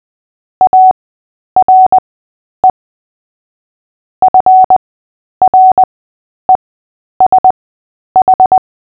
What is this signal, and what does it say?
Morse code, heard as a keyed tone, about 20 wpm, spaced as for 10 wpm, 750 Hz. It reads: ARE FRESH
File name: cw / fists / Short-Story-8-10wpm